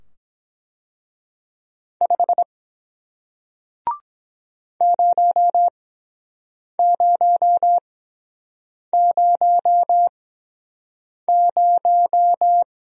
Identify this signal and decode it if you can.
Morse code, heard as a keyed tone, about 23 wpm, spaced as for 7 wpm, 700 Hz. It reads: 5 0000